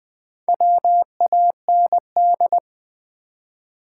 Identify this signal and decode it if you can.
Morse code, heard as a keyed tone, about 20 wpm, 700 Hz. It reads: WAND
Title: Build A Word - 2 Letters At A Time - 6-Letter Words - 20wpm